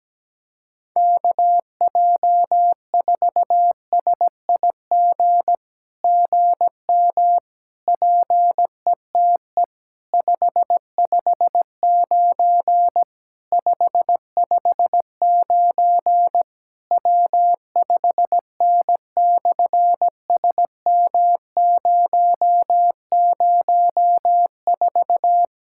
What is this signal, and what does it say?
Morse code, heard as a keyed tone, about 17 wpm, 700 Hz. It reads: KJ4SIG GM PETE 559 559 W5N/SM004
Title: SOTA QSO Practice 17wpm